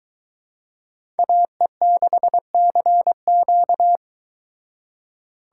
Morse code, heard as a keyed tone, about 23 wpm, 700 Hz. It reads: AE6CQ